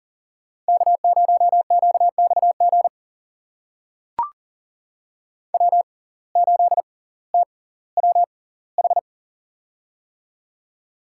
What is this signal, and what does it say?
Morse code, heard as a keyed tone, about 40 wpm, spaced as for 20 wpm, 700 Hz. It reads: K0QXG W8TWH
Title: USA Call Signs - 5 Characters - 6x Character Spacing - 40wpm